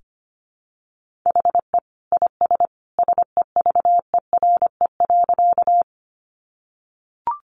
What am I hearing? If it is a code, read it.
Morse code, heard as a keyed tone, about 25 wpm, 700 Hz. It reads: HE IS SEVERE.
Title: Sentences from Top 1400 Words 25wpm